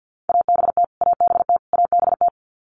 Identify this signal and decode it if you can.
Morse code, heard as a keyed tone, about 50 wpm, 700 Hz. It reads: ABT ABT ABT
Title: Abbreviations Repeated 3x - 50wpm